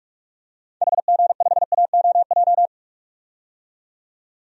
Morse code, heard as a keyed tone, about 46 wpm, 700 Hz. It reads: HG5AOJ
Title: DXCC-100 Call Signs - 6 Characters - 45wpm